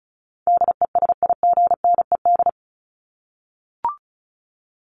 Morse code, heard as a keyed tone, about 34 wpm, 700 Hz. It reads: DESIGNED
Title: ED Suffix - Top 100 Words with Base Words 35wpm